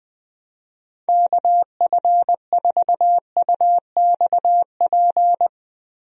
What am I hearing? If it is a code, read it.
Morse code, heard as a keyed tone, about 19 wpm, 700 Hz. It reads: KF4UXP
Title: USA Call Signs - 6 Characters - Repeated 3x - 20wpm